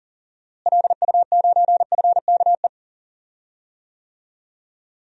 Morse code, heard as a keyed tone, about 40 wpm, 700 Hz. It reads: LU9FKE
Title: DXCC-100 Call Signs - 6 Characters - Repeated 3x - 40wpm